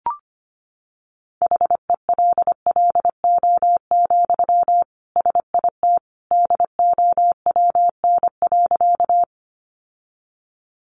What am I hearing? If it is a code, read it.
Morse code, heard as a keyed tone, about 25 wpm, 700 Hz. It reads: HELLO, SIT DOWN.
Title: Sentences from Top 1100 Words 25wpm